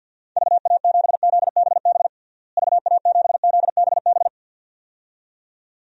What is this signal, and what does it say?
Morse code, heard as a keyed tone, about 52 wpm, 700 Hz. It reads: VA7ZBB VA7ZBB
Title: DXCC-100 Call Signs - 6 Characters - Repeated 2x - 50wpm